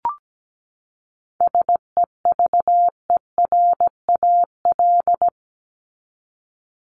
Morse code, heard as a keyed tone, about 17 wpm, 700 Hz. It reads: SEVERAL
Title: Top 500 Words 17wpm